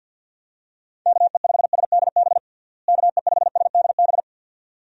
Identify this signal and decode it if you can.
Morse code, heard as a keyed tone, about 48 wpm, 700 Hz. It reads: KE5SDB KE5SDB
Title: USA Call Signs - 6 Characters - Repeated 2x - 50wpm